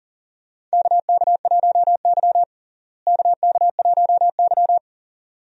Morse code, heard as a keyed tone, about 40 wpm, 700 Hz. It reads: KK1Y KK1Y
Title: USA Call Signs - 4 Characters - Repeated 2x - 40wpm